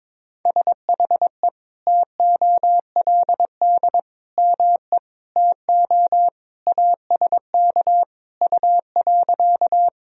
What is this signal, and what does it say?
Morse code, heard as a keyed tone, about 22 wpm, 700 Hz. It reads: SHE TOLD ME TO ASK U.
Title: Sentences from Top 400 Words and Abbreviations 22wpm